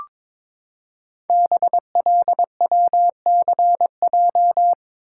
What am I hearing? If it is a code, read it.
Morse code, heard as a keyed tone, about 22 wpm, 700 Hz. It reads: BLWCJ
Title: Code Groups - 5 Letters - 22wpm